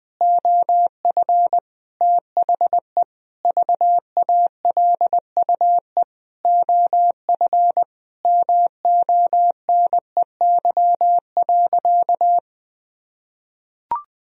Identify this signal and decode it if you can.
Morse code, heard as a keyed tone, about 20 wpm, 700 Hz. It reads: OF THE VALUE OF MONEY.
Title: Sentences from Top 1000 Words - Review Part 3 - 20wpm